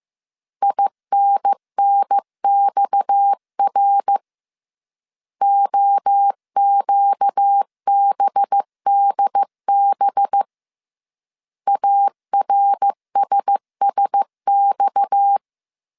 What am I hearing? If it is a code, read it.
Morse code, heard as a keyed tone, about 15 wpm, 800 Hz. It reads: INNXR OQBDB ARSSX